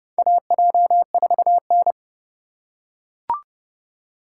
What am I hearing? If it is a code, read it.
Morse code, heard as a keyed tone, about 30 wpm, 700 Hz. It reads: AJ4N
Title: USA Call Signs - 4x Character Spacing - 30wpm